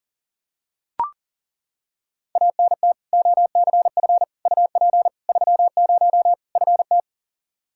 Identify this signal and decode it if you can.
Morse code, heard as a keyed tone, about 40 wpm, 700 Hz. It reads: ANT OCF UP 30 FT
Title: Practice QSO - Exchange 2 - Info - 40wpm